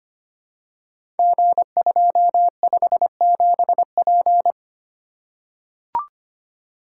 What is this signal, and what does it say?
Morse code, heard as a keyed tone, about 25 wpm, 700 Hz. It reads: G257P
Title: Code Groups - 5 Letter:Numbers - 25wpm